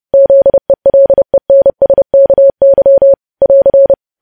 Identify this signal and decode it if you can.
Morse code, heard as a keyed tone, about 30 wpm, 550 Hz. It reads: ZELENSKY <AR>
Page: news headlines in morse code at 30 wpm